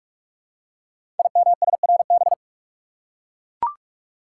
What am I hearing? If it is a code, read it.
Morse code, heard as a keyed tone, about 44 wpm, 700 Hz. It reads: IMSRB
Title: Code Groups - 5 Letters - 45wpm